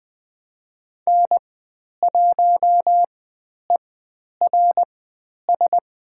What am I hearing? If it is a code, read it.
Morse code, heard as a keyed tone, about 20 wpm, spaced as for 10 wpm, 700 Hz. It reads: N1ERS
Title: 095 Lesson 16g - Letter c - Call Signs 20wpm